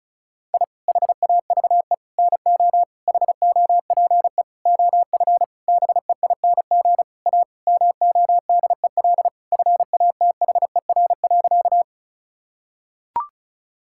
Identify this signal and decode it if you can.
Morse code, heard as a keyed tone, about 36 wpm, 700 Hz. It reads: I HAVE NO HOPE OF BEING A MODEL FATHER.
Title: Sentences from Top 1000 Words - Review Part 5 - 35wpm